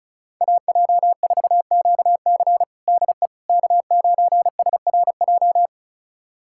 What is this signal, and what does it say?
Morse code, heard as a keyed tone, about 34 wpm, 700 Hz. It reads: AJ4QC DE K9SRJ
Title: USA Callsign DE Callsign - 5 Characters - 35wpm